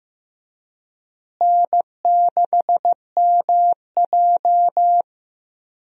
Morse code, heard as a keyed tone, about 15 wpm, 700 Hz. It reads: N6MJ
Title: USA Call Signs - 4 Characters - Repeated 3x - 15wpm